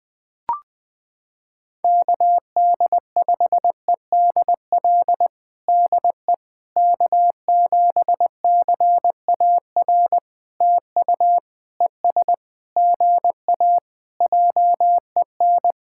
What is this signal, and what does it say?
Morse code, heard as a keyed tone, about 20 wpm, 700 Hz. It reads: KD5EDL DE K7CAR TU ES GA JEN
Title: POTA QSO Practice - 20wpm